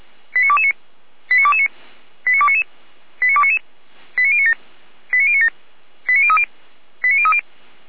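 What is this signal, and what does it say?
Five-tone selective call (ZVEI), heard as quick runs of five tones, tones of about 70 ms. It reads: 89229 89229 89299 89299 89097 89097 89030 89030
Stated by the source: ZVEI five-tone call received by SDR, from the server of feuerwehr-orlamuende (fire brigade Orlamünde).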